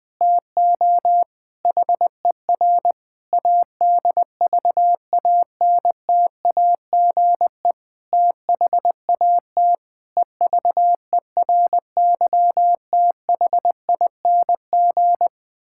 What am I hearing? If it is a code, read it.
Morse code, heard as a keyed tone, about 20 wpm, 700 Hz. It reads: TO HER ADVANTAGE THAT EVERYTHING
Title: Sentences from Top 1000 Words - Review Part 6 - 20wpm